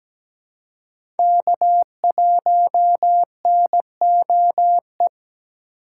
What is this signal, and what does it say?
Morse code, heard as a keyed tone, about 17 wpm, 700 Hz. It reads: K1NOE
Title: USA Call Signs - 2x Character Spacing - 17wpm